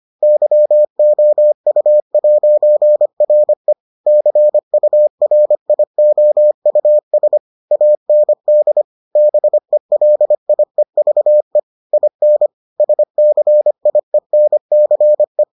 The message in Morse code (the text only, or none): YOU'RE CURIOUS AND BELIEVE IN SCIENCE